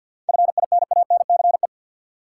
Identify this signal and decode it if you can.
Morse code, heard as a keyed tone, about 47 wpm, 700 Hz. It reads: FINANCE
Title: Sentences from Top 1500 Words - New Word Drill Alt - 50wpm